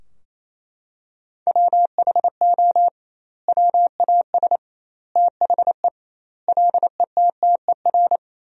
Morse code, heard as a keyed tone, about 28 wpm, 700 Hz. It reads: WHO WAS THE LETTER
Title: 2x Word-Spacing - Sentences from Top 300 Words - 28wpm